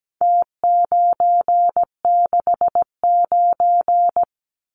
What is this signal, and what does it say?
Morse code, heard as a keyed tone, about 17 wpm, 700 Hz. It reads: T969
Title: Numbers - Contest Style - 4-Digit 17wpm